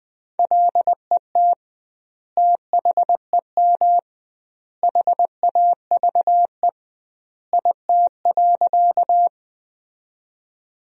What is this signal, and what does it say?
Morse code, heard as a keyed tone, about 20 wpm, 700 Hz. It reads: LET THEM HAVE IT.
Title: 2x Word-Spacing - Sentences from Top 300 Words - 20wpm